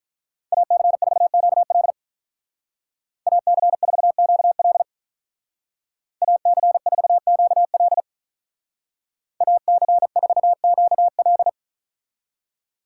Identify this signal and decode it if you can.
Morse code, heard as a keyed tone, about 43 wpm, 700 Hz. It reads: AC4QL AC4QL AC4QL AC4QL
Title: Speed Racing - USA 2x2 Call Signs 35wpm